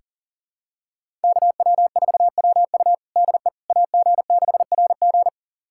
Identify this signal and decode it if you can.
Morse code, heard as a keyed tone, about 37 wpm, 700 Hz. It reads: KW4WU DE AG6RG
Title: USA Callsign DE Callsign - 5 Characters - 40wpm